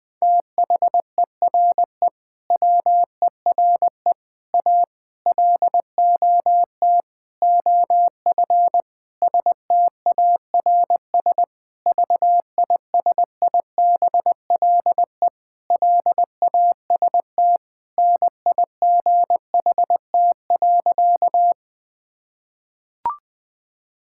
Morse code, heard as a keyed tone, about 20 wpm, 700 Hz. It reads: THERE WERE A LOT OF STARS VISIBLE LAST NIGHT.
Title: Sentences with 5-Word Phrases - 20wpm